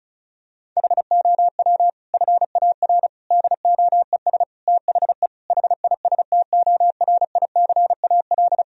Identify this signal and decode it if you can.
Morse code, heard as a keyed tone, about 36 wpm, 700 Hz. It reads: HOW FAR DOES THE HISTORICAL
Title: Sentences from Top 1000 Words - Review Part 4 - 35wpm